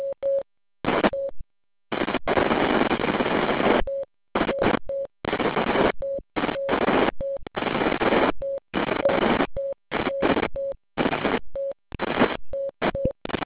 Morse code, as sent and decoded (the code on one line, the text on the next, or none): none